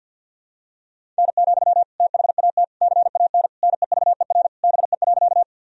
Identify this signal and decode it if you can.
Morse code, heard as a keyed tone, about 50 wpm, 700 Hz. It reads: N, THAT CAN NEVER BE.
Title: Sentences from Top 300 Words and Abbreviations 50wpm